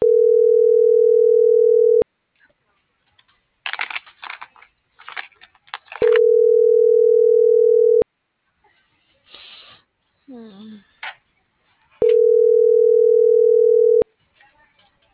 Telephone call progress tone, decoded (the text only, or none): ringback tone